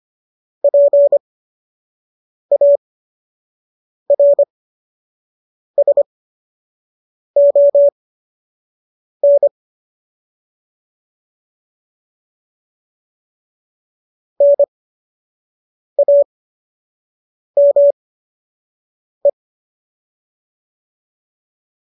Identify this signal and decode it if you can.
Morse code, heard as a keyed tone, about 25 wpm, spaced as for 6 wpm, 570 Hz. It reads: PARSON NAME
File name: qso003 06